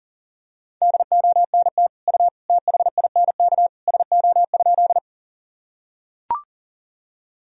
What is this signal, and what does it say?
Morse code, heard as a keyed tone, about 40 wpm, 700 Hz. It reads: DONT U THINK SO?
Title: Sentences from Top 300 Words and Abbreviations 40wpm